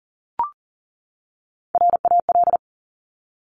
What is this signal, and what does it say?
Morse code, heard as a keyed tone, about 40 wpm, 700 Hz. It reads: RAL
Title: Top Three Letter Combinations in Words 40wpm